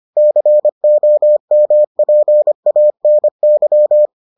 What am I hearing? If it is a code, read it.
Morse code, heard as a keyed tone, about 25 wpm, 600 Hz. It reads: COMPANY